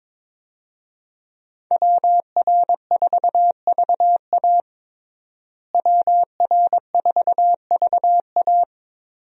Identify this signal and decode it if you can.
Morse code, heard as a keyed tone, about 22 wpm, 700 Hz. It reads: WR4VA WR4VA
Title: USA Call Signs - 5 Characters - Repeated 2x - 22wpm